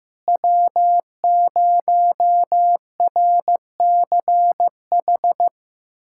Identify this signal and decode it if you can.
Morse code, heard as a keyed tone, about 15 wpm, 700 Hz. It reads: W0RCH
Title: USA Callsign DE Callsign - 5 Characters - 15wpm